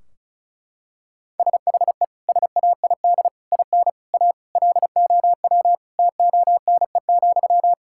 Morse code, heard as a keyed tone, about 34 wpm, 700 Hz. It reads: SHE SAID IN A LOW TONE,
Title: Sentences from Top 1600 Words 35wpm